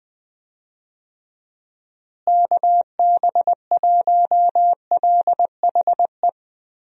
Morse code, heard as a keyed tone, about 20 wpm, 700 Hz. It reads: KB1LHE
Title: USA Call Signs - 6 Characters - Repeated 2x - 20wpm